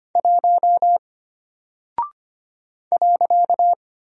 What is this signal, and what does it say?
Morse code, heard as a keyed tone, about 25 wpm, 700 Hz. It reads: 1 .
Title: Single Character (Letter:Number:Punctuation:Prosign) Alt 25wpm